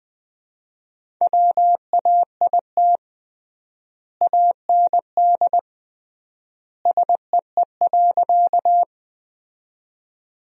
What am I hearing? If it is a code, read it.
Morse code, heard as a keyed tone, about 21 wpm, 700 Hz. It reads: WAIT AND SEE.
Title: 3x Word-Spacing - Sentences from Top 500 Words - 20wpm